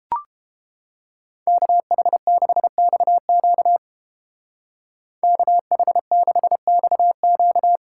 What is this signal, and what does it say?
Morse code, heard as a keyed tone, about 32 wpm, 700 Hz. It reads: KH6XQ KH6XQ